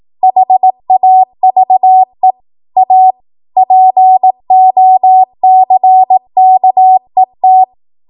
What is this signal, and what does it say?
Morse code, heard as a keyed tone, about 18 wpm, 750 Hz. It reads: HAVE A POCKET
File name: Archive / 18 WPM / 18WPM